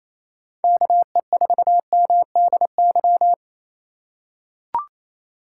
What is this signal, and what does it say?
Morse code, heard as a keyed tone, about 28 wpm, 700 Hz. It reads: KE4MDY
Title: USA Call Signs - 4x Character Spacing - 28wpm